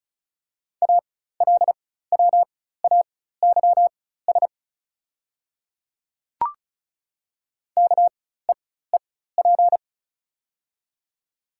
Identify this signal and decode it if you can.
Morse code, heard as a keyed tone, about 36 wpm, spaced as for 16 wpm, 690 Hz. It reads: ALWAYS KEEP
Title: 4x Character Spacing - Top 300 Words - 35wpm